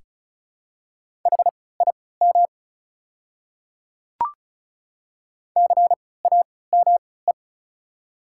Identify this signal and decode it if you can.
Morse code, heard as a keyed tone, about 36 wpm, spaced as for 20 wpm, 700 Hz. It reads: HIM CAME
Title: 3x Character Spacing - Top 300 Words - 35wpm